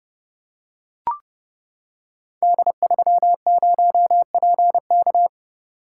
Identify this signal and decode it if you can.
Morse code, heard as a keyed tone, about 30 wpm, 700 Hz. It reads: D30PK
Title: Code Groups - 5 Letter:Numbers - 30wpm